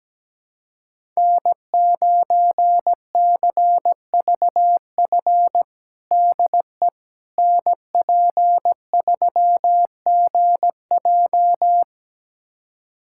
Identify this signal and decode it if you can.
Morse code, heard as a keyed tone, about 17 wpm, 700 Hz. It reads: N9CVF DE NP3GJ
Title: USA Callsign DE Callsign - 5 Characters - 17wpm